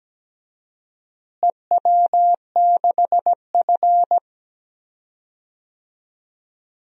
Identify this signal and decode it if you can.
Morse code, heard as a keyed tone, about 17 wpm, 700 Hz. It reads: EW6F
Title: DXCC-100 Call Signs - 4 Characters - Repeated 2x - 17wpm